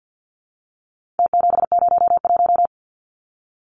Morse code, heard as a keyed tone, about 50 wpm, 700 Hz. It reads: T701
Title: Numbers - Contest Style - 4-Digit 50wpm